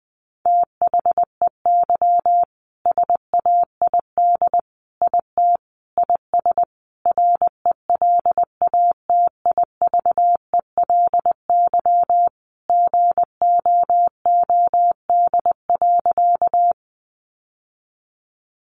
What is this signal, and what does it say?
Morse code, heard as a keyed tone, about 20 wpm, 700 Hz. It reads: THEY SAID IT IS RELATIVELY GOOD.
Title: Sentences from Top 1100 Words 20wpm